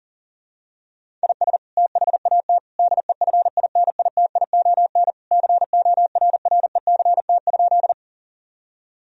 Morse code, heard as a keyed tone, about 40 wpm, 700 Hz. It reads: IS THAT DEFINITION CORRECT?
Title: Sentences from Top 1000 Words - Review Part 4 - 40wpm